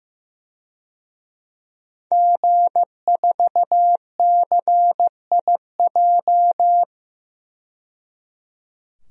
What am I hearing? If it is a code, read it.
Morse code, heard as a keyed tone, about 15 wpm, 700 Hz. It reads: G4CIJ
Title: DXCC-100 Call Signs - 5 Characters - 15wpm